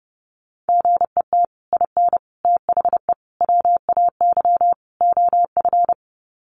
Morse code, heard as a keyed tone, about 30 wpm, 700 Hz. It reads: GET IN THE WAY OF
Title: Sets of 5 Words Encore 30wpm